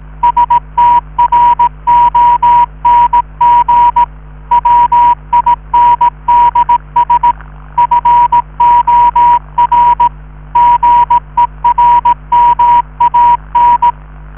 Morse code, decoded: STRONG WINDS FOR GERMAN